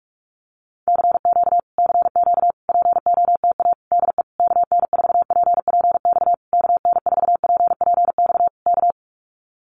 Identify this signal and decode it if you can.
Morse code, heard as a keyed tone, about 44 wpm, 700 Hz. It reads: CQ CQ POTA DE KN4PPX KN4PPX K
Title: Common QSO Phrases Encore 45wpm